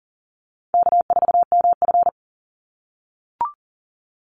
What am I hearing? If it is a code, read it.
Morse code, heard as a keyed tone, about 40 wpm, 700 Hz. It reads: K4MF
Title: NA SOTA Chaser Callsigns 40wpm